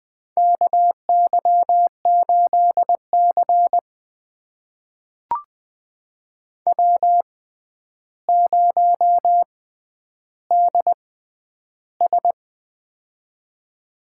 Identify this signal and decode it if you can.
Morse code, heard as a keyed tone, about 20 wpm, spaced as for 10 wpm, 700 Hz. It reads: KY8C W0DS
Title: USA Call Signs - 4 Characters - 6x Character Spacing - 20wpm